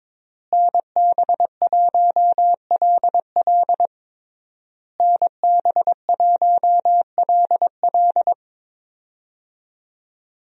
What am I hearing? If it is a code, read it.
Morse code, heard as a keyed tone, about 21 wpm, 700 Hz. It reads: NB1LL NB1LL
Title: USA Call Signs - 5 Characters - Repeated 3x - 22wpm